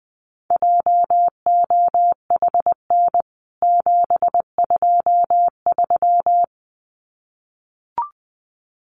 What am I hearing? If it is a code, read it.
Morse code, heard as a keyed tone, about 19 wpm, 700 Hz. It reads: JOHN 723
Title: CWT Exchange 20wpm